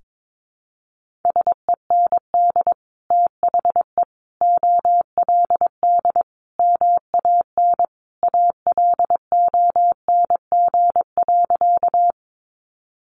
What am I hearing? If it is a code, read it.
Morse code, heard as a keyed tone, about 22 wpm, 700 Hz. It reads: SEND THE OLD MAN ALONG.
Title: Sentences from Top 1500 Words 22wpm